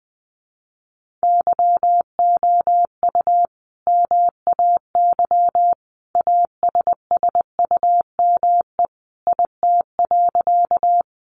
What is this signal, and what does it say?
Morse code, heard as a keyed tone, about 20 wpm, 700 Hz. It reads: YOU MAY ASSUME IT.
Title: Sentences from Top 1500 Words 20wpm